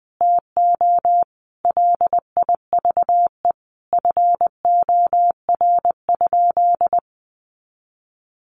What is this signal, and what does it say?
Morse code, heard as a keyed tone, about 20 wpm, 700 Hz. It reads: TO LIVE FOR?
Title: Sentences from Top 200 Words 20wpm